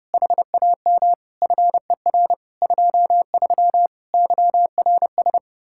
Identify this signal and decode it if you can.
Morse code, heard as a keyed tone, about 29 wpm, 700 Hz. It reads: HAM FER 23 YRS